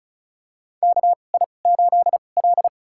Morse code, heard as a keyed tone, about 36 wpm, spaced as for 25 wpm, 700 Hz. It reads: KI8L
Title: USA Call Signs - 4 Characters - 2x Character Spacing - 35wpm